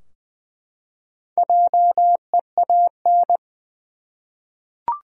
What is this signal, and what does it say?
Morse code, heard as a keyed tone, about 20 wpm, 700 Hz. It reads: JEAN
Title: USA Common Names - Max 4 Letters - 20wpm